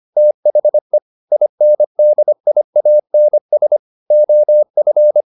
Morse code, heard as a keyed tone, about 25 wpm, 600 Hz. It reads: THE INDIANS OF
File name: the old man and the sea part 1 25wpm